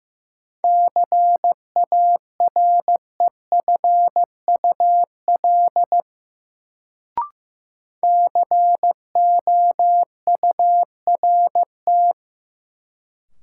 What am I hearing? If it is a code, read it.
Morse code, heard as a keyed tone, about 15 wpm, 700 Hz. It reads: CAREFUL COURT
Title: Sentences from Top 1600 Words - New Word Drill Alt - 15wpm